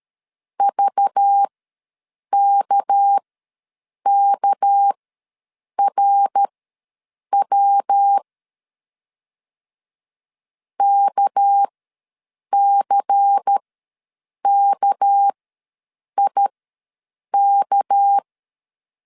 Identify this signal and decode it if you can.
Morse code, heard as a keyed tone, about 13 wpm, spaced as for 7 wpm, 800 Hz. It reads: VKKRW KCKIK
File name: LEZIONI / 6 PKX0 / 6 PK ws 12wpm 3116